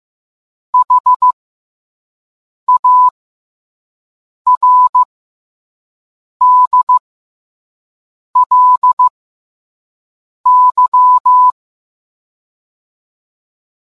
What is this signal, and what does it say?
Morse code, heard as a keyed tone, about 15 wpm, spaced as for 5 wpm, 1000 Hz. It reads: HARDLY